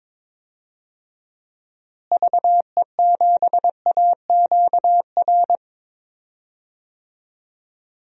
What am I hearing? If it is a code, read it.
Morse code, heard as a keyed tone, about 21 wpm, 700 Hz. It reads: VE7AQR
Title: DXCC-100 Call Signs - 6 Characters - Repeated 2x - 22wpm